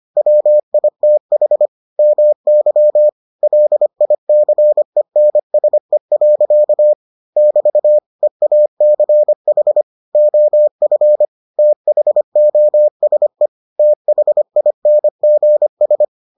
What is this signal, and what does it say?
Morse code, heard as a keyed tone, about 25 wpm, 600 Hz. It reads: WITH MY LICENSE. <BT> EACH OF THOSE THINGS